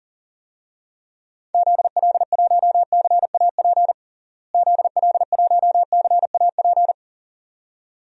Morse code, heard as a keyed tone, about 40 wpm, 700 Hz. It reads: ZL1CAP ZL1CAP
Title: DXCC-100 Call Signs - 6 Characters - Repeated 2x - 40wpm